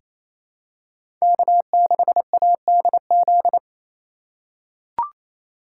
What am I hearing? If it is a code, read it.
Morse code, heard as a keyed tone, about 28 wpm, 700 Hz. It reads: K6ADZ
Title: USA Call Signs - 3x Character Spacing - 28wpm